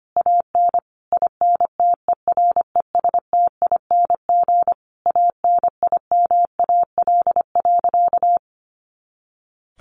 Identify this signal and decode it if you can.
Morse code, heard as a keyed tone, about 25 wpm, 700 Hz. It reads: AN INTERESTING ANIMAL.
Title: Easy Phrases 25wpm